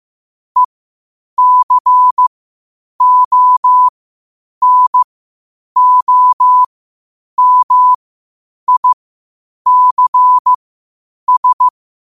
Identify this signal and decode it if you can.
Morse code, heard as a keyed tone, about 15 wpm, spaced as for 8 wpm, 1000 Hz. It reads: ECONOMICS
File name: qotdfiles / 8wpm